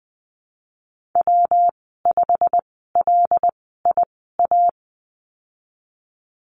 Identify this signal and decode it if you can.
Morse code, heard as a keyed tone, about 20 wpm, spaced as for 15 wpm, 700 Hz. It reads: W5LIA
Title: 128 Lesson 21g - Number 6 - Call Signs 20wpm